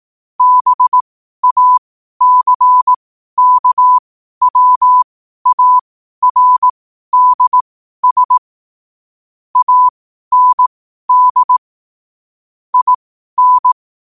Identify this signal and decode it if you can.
Morse code, heard as a keyed tone, about 18 wpm, spaced as for 12 wpm, 1000 Hz. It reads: BACKWARDS AND IN
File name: qotdfiles / 12wpm